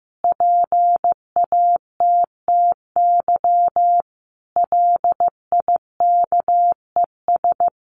Morse code, heard as a keyed tone, about 15 wpm, 700 Hz. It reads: PATTY LIKES
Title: Additional Easy Phrases 15wpm